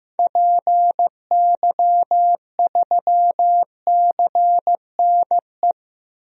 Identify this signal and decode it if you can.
Morse code, heard as a keyed tone, about 15 wpm, 700 Hz. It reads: PY3CNE
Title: DXCC-100 Call Signs - 6 Characters - Repeated 3x - 15wpm